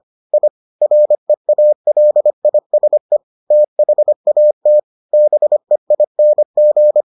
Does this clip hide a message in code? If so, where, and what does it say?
Yes: Morse code from 0.3 to 7.0 s, heard as a keyed tone, about 25 wpm, 600 Hz. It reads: I REALISE THAT BEING